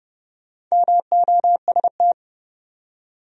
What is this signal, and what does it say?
Morse code, heard as a keyed tone, about 42 wpm, spaced as for 36 wpm, 700 Hz. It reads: MOST
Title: Speed-Racing Top 300 Words 30wpm